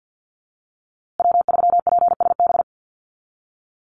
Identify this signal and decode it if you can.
Morse code, heard as a keyed tone, about 50 wpm, 700 Hz. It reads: W3PSB
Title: USA Call Signs Repeated 2x 50wpm